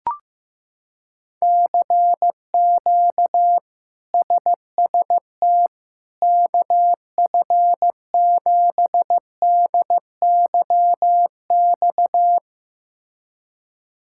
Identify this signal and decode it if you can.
Morse code, heard as a keyed tone, about 15 wpm, 700 Hz. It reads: CQ SST KF7DYX
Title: K1USN SST QSO Practice - 15wpm